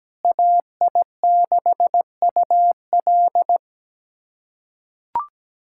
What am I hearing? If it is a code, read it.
Morse code, heard as a keyed tone, about 17 wpm, 700 Hz. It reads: AI6UL
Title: USA Callsign DE Callsign - 5 Characters - 17wpm